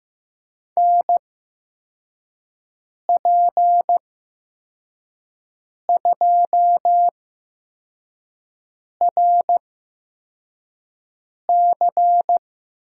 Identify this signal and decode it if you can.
Morse code, heard as a keyed tone, about 15 wpm, spaced as for 4 wpm, 700 Hz. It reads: NP2RC
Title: USA Call Signs - 5 Characters - 8x Character Spacing - 15wpm